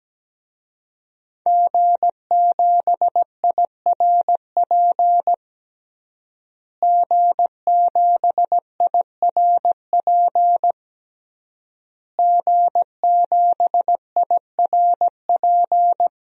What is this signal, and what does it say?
Morse code, heard as a keyed tone, about 17 wpm, 700 Hz. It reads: G7IRP G7IRP G7IRP
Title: DXCC-100 Call Signs - 5 Characters - Repeated 3x - 17wpm